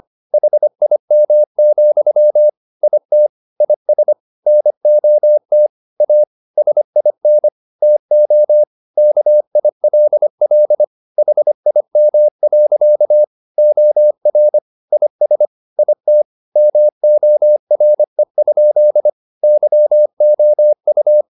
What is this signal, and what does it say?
Morse code, heard as a keyed tone, about 25 wpm, 600 Hz. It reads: HIM, IT IS NOT A SIN TO KILL HIM. OR IS IT MORE? YOU